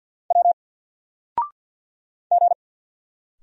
Morse code, heard as a keyed tone, about 50 wpm, 700 Hz. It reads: W G
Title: Single Letter-Number 50wpm